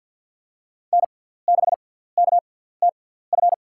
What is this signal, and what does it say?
Morse code, heard as a keyed tone, about 48 wpm, spaced as for 17 wpm, 690 Hz. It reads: N6KTF
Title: USA Call Signs - 6x Character Spacing - 50wpm